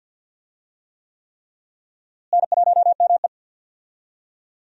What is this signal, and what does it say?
Morse code, heard as a keyed tone, about 52 wpm, 700 Hz. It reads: N1ME
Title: USA Call Signs - 4 Characters - Repeated 3x - 50wpm